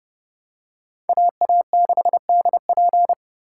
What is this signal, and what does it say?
Morse code, heard as a keyed tone, about 28 wpm, 700 Hz. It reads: AA6DP